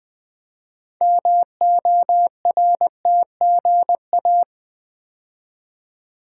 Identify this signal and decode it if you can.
Morse code, heard as a keyed tone, about 20 wpm, 700 Hz. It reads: MORTGA
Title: Build A Word - 2 Letters At A Time - 8-Letter Words - 20wpm